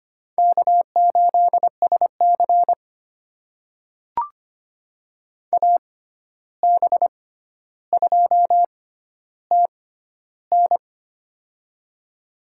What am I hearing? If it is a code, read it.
Morse code, heard as a keyed tone, about 25 wpm, spaced as for 12 wpm, 700 Hz. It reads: K8SC AB2TN
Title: USA Call Signs - 6x Character Spacing - 25wpm